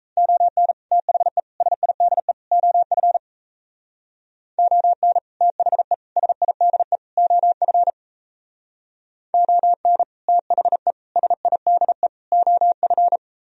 Morse code, heard as a keyed tone, about 36 wpm, 700 Hz. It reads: ON THE SIDE OF ON THE SIDE OF ON THE SIDE OF